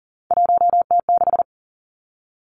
Morse code, heard as a keyed tone, about 42 wpm, 700 Hz. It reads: 1T6
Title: Three Challenging Letter-Number Combinations 40wpm